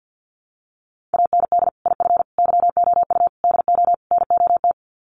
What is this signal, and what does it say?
Morse code, heard as a keyed tone, about 50 wpm, 700 Hz. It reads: AND IF YOU DO NOT
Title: Sets of 5 Words 50wpm